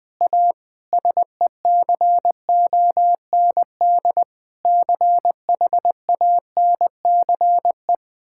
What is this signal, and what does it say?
Morse code, heard as a keyed tone, about 20 wpm, 700 Hz. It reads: A SECOND CHANCE
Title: Difficult Sentences with Hints - 20wpm